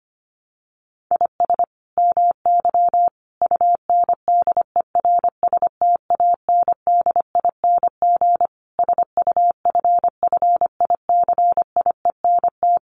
Morse code, heard as a keyed tone, about 25 wpm, 700 Hz. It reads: IS MY UNDERSTANDING SUFFICIENT
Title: Sentences from Top 1400 Words 25wpm